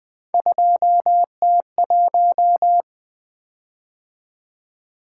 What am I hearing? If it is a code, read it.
Morse code, heard as a keyed tone, about 20 wpm, 700 Hz. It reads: 2T1
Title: Three Letter-Number Combinations - 20wpm